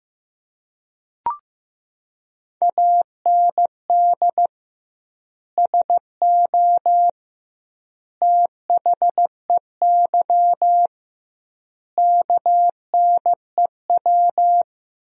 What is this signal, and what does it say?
Morse code, heard as a keyed tone, about 15 wpm, 700 Hz. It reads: AND SO THEY KNEW
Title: 2x Word-Spacing - Sentences from Top 400 Words - 15wpm